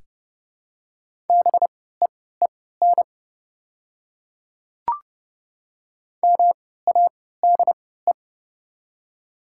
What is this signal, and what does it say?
Morse code, heard as a keyed tone, about 28 wpm, spaced as for 17 wpm, 690 Hz. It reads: BEEN MADE
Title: 3x Character Spacing - Top 300 Words - 30wpm